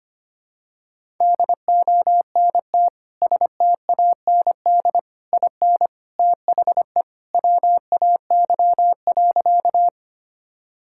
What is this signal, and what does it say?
Morse code, heard as a keyed tone, about 25 wpm, 700 Hz. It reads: DONT STAND IN THE WAY.
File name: practice / 25wpm / Sentences with 3-Word Phrases - 25wpm